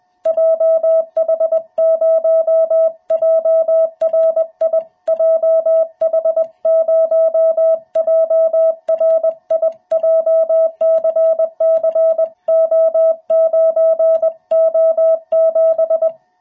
Morse code, heard as a keyed tone, about 21 wpm, 660 Hz. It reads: JH0JRI JH0JRI JCCO9O7
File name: jh8cbh / 2021cw